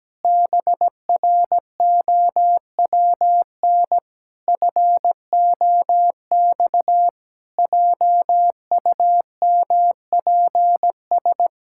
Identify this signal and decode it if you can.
Morse code, heard as a keyed tone, about 17 wpm, 700 Hz. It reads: BROWN FOX JUMPS
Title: Pangram Challenge v2 - 17wpm